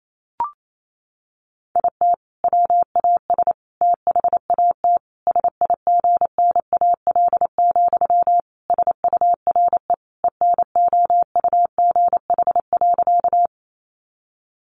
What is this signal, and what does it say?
Morse code, heard as a keyed tone, about 28 wpm, 700 Hz. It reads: IT WAS THAT SIGNAL, SURE ENOUGH.
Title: Sentences from Top 1400 Words 28wpm